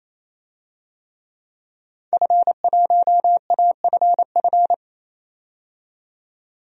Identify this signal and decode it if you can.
Morse code, heard as a keyed tone, about 28 wpm, 700 Hz. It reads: F1AFF
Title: DXCC-100 Call Signs - 5 Characters - 28wpm